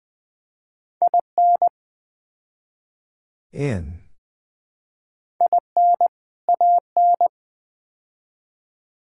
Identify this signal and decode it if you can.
Morse code, heard as a keyed tone, about 20 wpm, 700 Hz. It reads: IN IN AN
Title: Build A Phrase - 4-Word Phrases - 20wpm